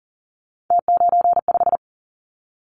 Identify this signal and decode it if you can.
Morse code, heard as a keyed tone, about 42 wpm, 700 Hz. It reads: T95
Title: Callsign Prefixes for Top 100 DXCC - 40wpm